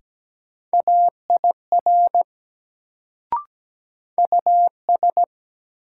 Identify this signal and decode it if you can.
Morse code, heard as a keyed tone, about 17 wpm, 700 Hz. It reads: AIR US
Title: Sentences from Top 200 Words - New Word Drill - 17wpm